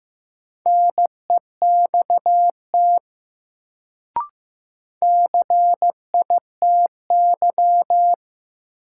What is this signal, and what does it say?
Morse code, heard as a keyed tone, about 15 wpm, 700 Hz. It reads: NEXT CITY
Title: Sentences from Top 300 Words - New Word Drill - 15wpm